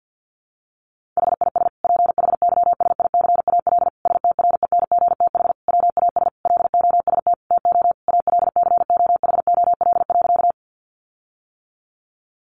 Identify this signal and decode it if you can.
Morse code, heard as a keyed tone, about 50 wpm, 700 Hz. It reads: HIS PHYSICAL STRENGTH WAS LOST TO ALCOHOL.
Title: Sentences from Top 1400 Words 50wpm